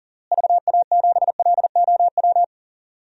Morse code, heard as a keyed tone, about 40 wpm, 700 Hz. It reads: VA7LOW
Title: DXCC-100 Call Signs - 6 Characters - Repeated 3x - 40wpm